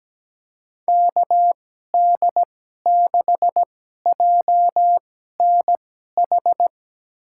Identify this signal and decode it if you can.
Morse code, heard as a keyed tone, about 17 wpm, spaced as for 12 wpm, 700 Hz. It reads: KD6JNH
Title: USA Call Signs - 6 Characters - 2x Character Spacing - 17wpm